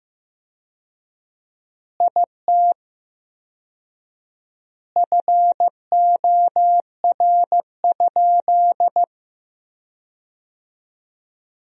Morse code, heard as a keyed tone, about 16 wpm, 700 Hz. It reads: IT FOR?
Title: 4x Word-Spacing - Sentences from Top 100 Words - 15wpm